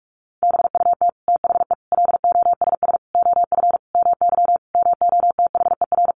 Morse code, heard as a keyed tone, about 44 wpm, 700 Hz. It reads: BUT THE LOSS OF MY MOTHER
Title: Sentences from Top 1400 Words 45wpm